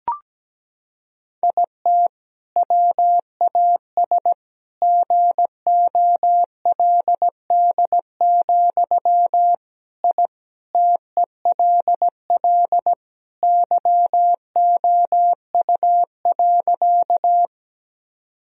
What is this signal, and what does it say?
Morse code, heard as a keyed tone, about 17 wpm, 700 Hz. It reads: IT WAS GOLD, I TELL YOU.
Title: Sentences from Top 1000 Words - Review Part 3 - 17wpm